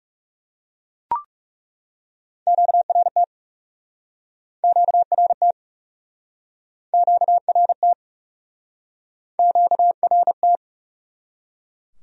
Morse code, heard as a keyed tone, about 37 wpm, 700 Hz. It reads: QRT QRT QRT QRT
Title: Speed-Racing Q-Codes - 30wpm